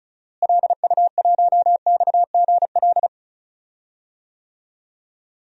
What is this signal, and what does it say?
Morse code, heard as a keyed tone, about 36 wpm, 700 Hz. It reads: LU1XGL
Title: DXCC-100 Call Signs - 6 Characters - Repeated 3x - 35wpm